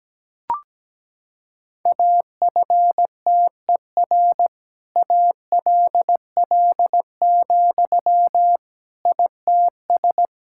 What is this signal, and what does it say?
Morse code, heard as a keyed tone, about 17 wpm, 700 Hz. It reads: AFTER ALL, ITS
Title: Sentences from Top 1000 Words - Review Part 5 - 17wpm